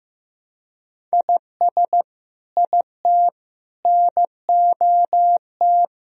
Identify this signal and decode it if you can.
Morse code, heard as a keyed tone, about 15 wpm, 700 Hz. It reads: IS IT NOT
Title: Sentences from Top 1000 Words - Review Part 2 - 15wpm